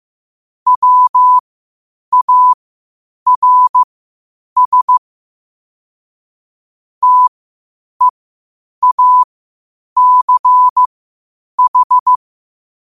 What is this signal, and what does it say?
Morse code, heard as a keyed tone, about 15 wpm, spaced as for 8 wpm, 1000 Hz. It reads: WARS TEACH
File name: qotdfiles / 8wpm